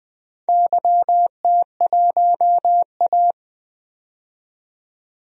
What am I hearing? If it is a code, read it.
Morse code, heard as a keyed tone, about 20 wpm, 700 Hz. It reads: YT1A
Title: DXCC-100 Call Signs - 4 Characters - 20wpm